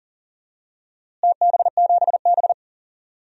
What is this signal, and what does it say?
Morse code, heard as a keyed tone, about 40 wpm, 700 Hz. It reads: TB7B
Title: DXCC-100 Call Signs - 4 Characters - Repeated 2x - 40wpm